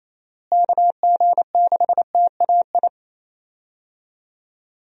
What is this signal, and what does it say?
Morse code, heard as a keyed tone, about 28 wpm, 700 Hz. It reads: KG6TAI
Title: USA Call Signs - 6 Characters - Repeated 2x - 28wpm